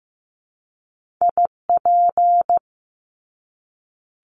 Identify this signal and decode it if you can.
Morse code, heard as a keyed tone, about 15 wpm, 700 Hz. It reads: IP
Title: Top Two Letter Combinations in Words 15wpm